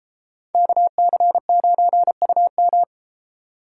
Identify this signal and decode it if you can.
Morse code, heard as a keyed tone, about 32 wpm, 700 Hz. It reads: KC9UM